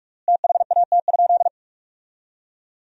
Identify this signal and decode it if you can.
Morse code, heard as a keyed tone, about 46 wpm, 700 Hz. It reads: THAT?
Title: 3x Word-Spacing - Sentences from Top 300 Words - 45wpm